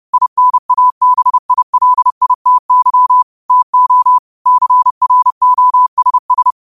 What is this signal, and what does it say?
Morse code, heard as a keyed tone, about 30 wpm, 1000 Hz. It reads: INABILITY TO CROSS
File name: qotdfiles / 30wpm